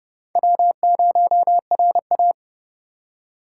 Morse code, heard as a keyed tone, about 29 wpm, 700 Hz. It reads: W0RA